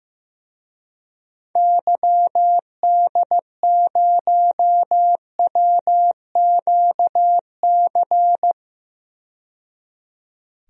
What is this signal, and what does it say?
Morse code, heard as a keyed tone, about 15 wpm, 700 Hz. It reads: YD0WQC
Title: DXCC-100 Call Signs - 6 Characters - 15wpm